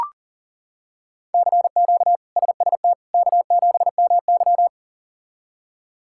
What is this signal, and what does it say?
Morse code, heard as a keyed tone, about 40 wpm, 700 Hz. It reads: CQ SST K7MY
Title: K1USN SST QSO Practice - 40wpm